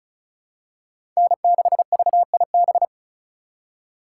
Morse code, heard as a keyed tone, about 34 wpm, 700 Hz. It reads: N6VIB
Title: USA Call Signs - 4x Character Spacing - 35wpm